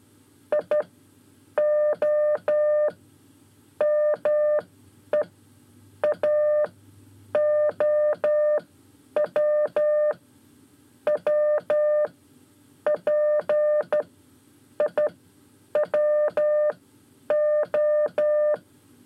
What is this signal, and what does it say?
Morse code, heard as a keyed tone, about 12 wpm, 600 Hz. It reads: I O M E A O W W P I W O
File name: 5 Lezione CW 1